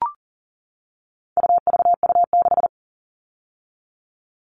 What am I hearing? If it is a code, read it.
Morse code, heard as a keyed tone, about 40 wpm, 700 Hz. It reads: UVU6
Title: All Cut Numbers - 4-Digit 40wpm